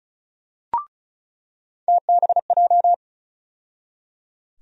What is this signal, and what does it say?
Morse code, heard as a keyed tone, about 36 wpm, 700 Hz. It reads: TBJ
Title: USA Call Sign Postfixes - 3 Letters 35wpm